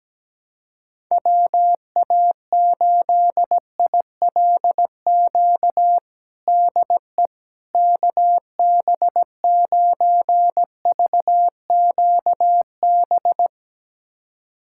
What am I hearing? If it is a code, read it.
Morse code, heard as a keyed tone, about 17 wpm, 700 Hz. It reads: WA8ILQ DE KB9VQB
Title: USA Callsign DE Callsign 17wpm